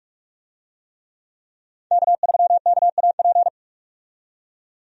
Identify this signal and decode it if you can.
Morse code, heard as a keyed tone, about 41 wpm, 700 Hz. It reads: K3KAP
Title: USA Call Signs - 5 Characters - 45wpm